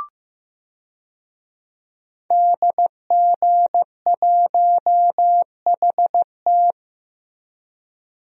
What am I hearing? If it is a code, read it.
Morse code, heard as a keyed tone, about 15 wpm, 700 Hz. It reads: DG1HT
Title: DXCC-100 Call Signs - 5 Characters - Repeated 3x - 15wpm